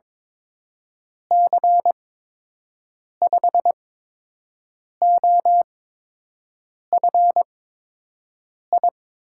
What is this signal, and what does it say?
Morse code, heard as a keyed tone, about 22 wpm, spaced as for 6 wpm, 700 Hz. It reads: C5OFI
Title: USA Call Signs - 8x Character Spacing - 22wpm